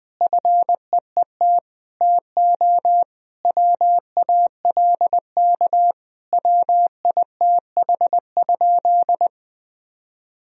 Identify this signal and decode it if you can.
Morse code, heard as a keyed tone, about 20 wpm, 700 Hz. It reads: FEET TO WALK WITH?
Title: Sentences from Top 1000 Words - Review Part 1 - 20wpm